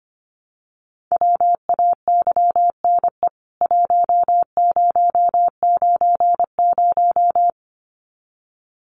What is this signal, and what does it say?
Morse code, heard as a keyed tone, about 25 wpm, 690 Hz. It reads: WAYNE 1090
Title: CWT Exchange 25wpm